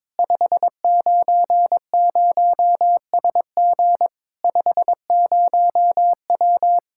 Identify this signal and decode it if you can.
Morse code, heard as a keyed tone, about 21 wpm, 700 Hz. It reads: 590SG 50W
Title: Practice QSO - Exchange 2 - Info - 22wpm